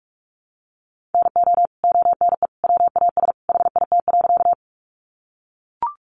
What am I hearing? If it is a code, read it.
Morse code, heard as a keyed tone, about 44 wpm, 700 Hz. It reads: NO ONE WAS HIT.
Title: Sentences from Top 1200 Words 45wpm